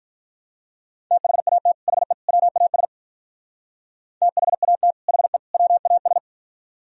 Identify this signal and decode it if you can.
Morse code, heard as a keyed tone, about 43 wpm, 700 Hz. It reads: THAT HE WAS THAT HE WAS
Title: Speed-Racing - Sets of 3 Words 35wpm